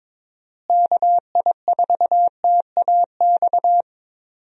Morse code, heard as a keyed tone, about 22 wpm, 700 Hz. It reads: KI4TAX
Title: USA Call Signs - 6 Characters - Repeated 3x - 22wpm